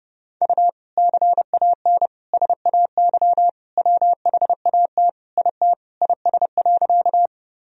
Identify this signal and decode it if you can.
Morse code, heard as a keyed tone, about 30 wpm, 700 Hz. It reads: U CAN SAY WHAT IT IS.
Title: Sentences from Top 200 Words and Abbreviations 30wpm